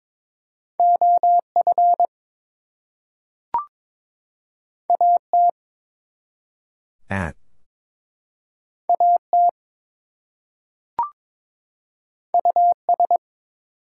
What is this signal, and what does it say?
Morse code, heard as a keyed tone, about 22 wpm, 700 Hz. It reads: OF AT AT US
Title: Two Letter Words - 22wpm